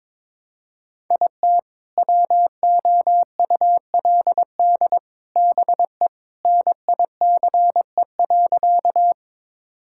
Morse code, heard as a keyed tone, about 22 wpm, 700 Hz. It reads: IT WOULD BE NICE.
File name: practice / 22wpm / Sentences with 3-Word Phrases - 22wpm